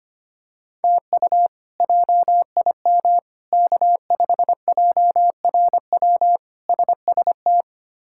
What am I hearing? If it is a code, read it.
Morse code, heard as a keyed tone, about 25 wpm, 700 Hz. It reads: TU JIM K5JRW SST
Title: K1USN SST QSO Practice - 25wpm